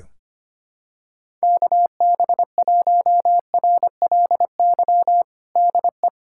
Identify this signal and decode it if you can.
Morse code, heard as a keyed tone, about 25 wpm, 700 Hz. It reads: KB1RLY DE
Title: USA Callsign DE Callsign 25wpm